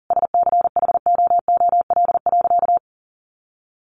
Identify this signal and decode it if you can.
Morse code, heard as a keyed tone, about 40 wpm, 700 Hz. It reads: SCHOOL.
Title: Sentences from Top 1100 Words 40wpm